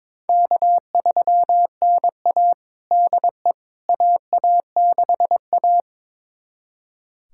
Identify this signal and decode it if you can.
Morse code, heard as a keyed tone, about 22 wpm, 700 Hz. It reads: K3NA DE AA6A
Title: USA Callsign DE Callsign - 4 Characters - 22wpm